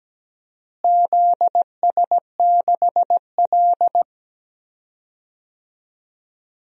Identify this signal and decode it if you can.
Morse code, heard as a keyed tone, about 17 wpm, 700 Hz. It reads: ZS6L
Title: DXCC-100 Call Signs - 4 Characters - Repeated 2x - 17wpm